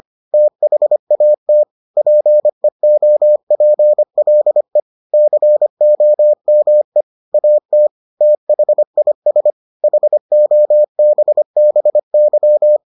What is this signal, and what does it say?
Morse code, heard as a keyed tone, about 25 wpm, 600 Hz. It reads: THAT PEOPLE COME AT THIS HOBBY